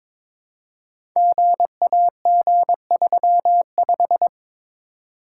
Morse code, heard as a keyed tone, about 22 wpm, 690 Hz. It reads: GAG35
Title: Code Groups - 5 Letter:Numbers - 22wpm